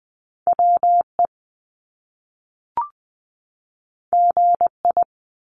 Morse code, heard as a keyed tone, about 20 wpm, 700 Hz. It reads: WE GI
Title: Top Two Letter Combinations in Words 20wpm